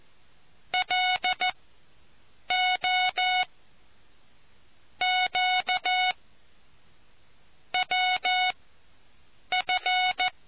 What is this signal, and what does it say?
Morse code, heard as a keyed tone, about 14 wpm, spaced as for 7 wpm, 740 Hz. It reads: LO Q WF